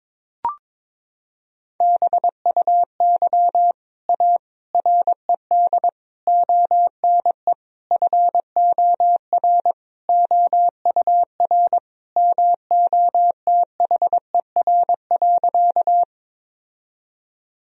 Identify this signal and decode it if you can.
Morse code, heard as a keyed tone, about 21 wpm, 700 Hz. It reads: BUY A RED ONE FOR OUR MOTHER.
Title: Sentences from Top 1000 Words - Review Part 2 - 22wpm